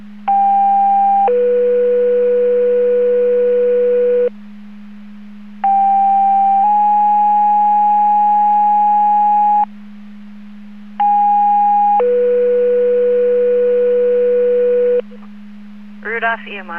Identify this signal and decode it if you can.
Two-tone page, heard as a one-second tone, then a three-second tone, about 800 then 490 Hz, about 790 then 830 Hz, about 820 then 490 Hz, with speech. It is heard from 0.3 to 15.0 s.